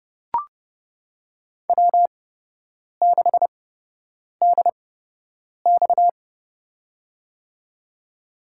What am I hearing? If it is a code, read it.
Morse code, heard as a keyed tone, about 30 wpm, spaced as for 8 wpm, 700 Hz. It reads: W6DX
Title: USA Call Signs - 4 Characters - 8x Character Spacing - 30wpm